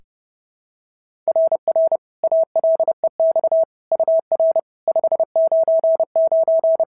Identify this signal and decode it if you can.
Morse code, heard as a keyed tone, about 30 wpm, 650 Hz. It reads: RR ALEX UR 599
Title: SOTA QSO Practice 30wpm